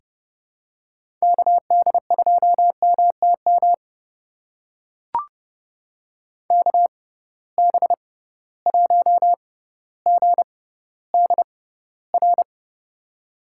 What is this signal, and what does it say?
Morse code, heard as a keyed tone, about 30 wpm, spaced as for 15 wpm, 700 Hz. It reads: KD2MTM KB1GDR
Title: USA Call Signs - 6 Characters - 6x Character Spacing - 30wpm